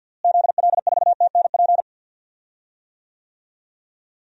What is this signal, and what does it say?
Morse code, heard as a keyed tone, about 45 wpm, 700 Hz. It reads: ZL4TNP
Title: DXCC-100 Call Signs - 6 Characters - Repeated 3x - 50wpm